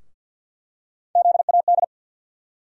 Morse code, heard as a keyed tone, about 51 wpm, 700 Hz. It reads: ZAD
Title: USA Call Sign Postfixes - 3 Letters 50wpm